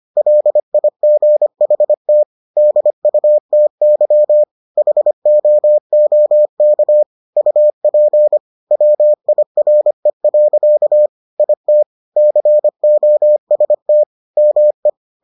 Morse code, heard as a keyed tone, about 25 wpm, 600 Hz. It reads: LIGHT DUTY HOOK UP WIRE. IT COST ME